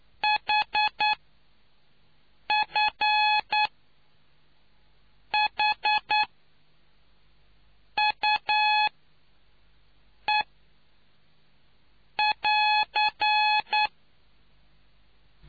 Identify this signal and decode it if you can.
Morse code, heard as a keyed tone, about 9 wpm, spaced as for 4 wpm, 850 Hz. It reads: HFHUE<AR>